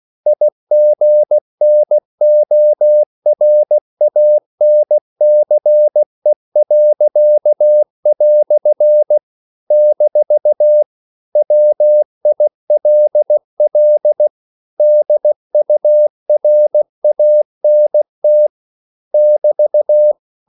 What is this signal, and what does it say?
Morse code, heard as a keyed tone, about 16 wpm, 600 Hz. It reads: IGNORANCE." - WILL DURANT <BT>
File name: CW / Practice / EnglishQuotations0216